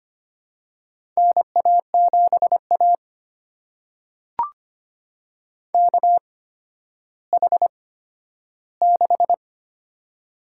Morse code, heard as a keyed tone, about 25 wpm, spaced as for 12 wpm, 700 Hz. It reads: NA7A KH6
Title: USA Call Signs - 4 Characters - 8x Character Spacing - 25wpm